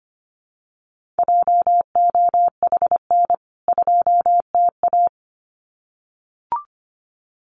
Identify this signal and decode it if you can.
Morse code, heard as a keyed tone, about 25 wpm, 700 Hz. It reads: JOHN 2TA
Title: CWT Exchange 25wpm